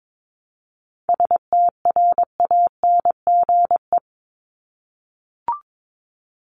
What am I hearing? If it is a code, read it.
Morse code, heard as a keyed tone, about 22 wpm, 700 Hz. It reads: STRANGE
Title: Top 1501-1750 Words 22wpm